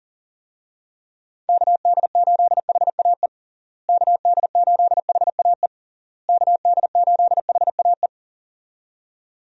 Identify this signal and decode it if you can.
Morse code, heard as a keyed tone, about 37 wpm, 700 Hz. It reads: KD8HAE KD8HAE KD8HAE
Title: USA Call Signs - 6 Characters - Repeated 3x - 40wpm